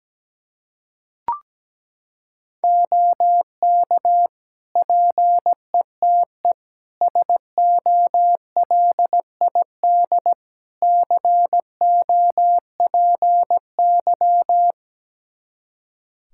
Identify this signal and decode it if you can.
Morse code, heard as a keyed tone, about 17 wpm, 700 Hz. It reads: OK PETE SOLID COPY
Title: Practice QSO - Exchange 3 - More Info - 17wpm